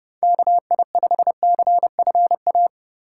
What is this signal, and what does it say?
Morse code, heard as a keyed tone, about 27 wpm, 700 Hz. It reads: KI5CFA